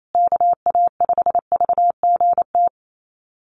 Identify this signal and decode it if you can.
Morse code, heard as a keyed tone, about 28 wpm, 700 Hz. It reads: KA5VGT
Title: USA Call Signs Repeated 3x 28wpm